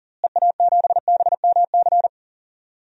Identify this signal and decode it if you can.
Morse code, heard as a keyed tone, about 40 wpm, 700 Hz. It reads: EA7BMC
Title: DXCC-100 Call Signs - 6 Characters - Repeated 2x - 40wpm